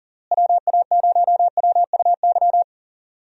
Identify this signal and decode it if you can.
Morse code, heard as a keyed tone, about 40 wpm, 700 Hz. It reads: WA0WUY
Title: USA Call Signs - 6 Characters - Repeated 3x - 40wpm